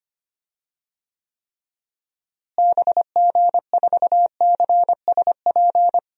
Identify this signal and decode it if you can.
Morse code, heard as a keyed tone, about 25 wpm, 700 Hz. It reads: BG4CSP
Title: DXCC-100 Call Signs - 6 Characters - Repeated 3x - 25wpm